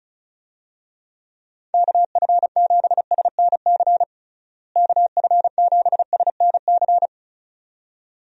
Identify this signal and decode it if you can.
Morse code, heard as a keyed tone, about 34 wpm, 700 Hz. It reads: KF7SNC KF7SNC
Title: USA Call Signs - 6 Characters - Repeated 2x - 35wpm